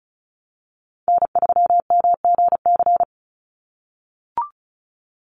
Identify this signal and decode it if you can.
Morse code, heard as a keyed tone, about 34 wpm, 700 Hz. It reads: N3MGC
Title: USA Call Signs 35wpm